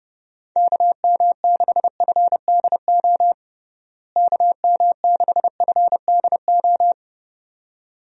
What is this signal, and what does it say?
Morse code, heard as a keyed tone, about 32 wpm, 700 Hz. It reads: KM6FDO KM6FDO
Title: USA Call Signs - 6 Characters - Repeated 2x - 30wpm